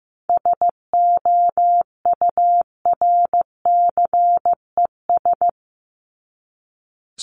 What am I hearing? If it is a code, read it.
Morse code, heard as a keyed tone, about 15 wpm, 700 Hz. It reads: SOURCES
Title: ES:S Suffix - Top 100 Words with Base Words 15wpm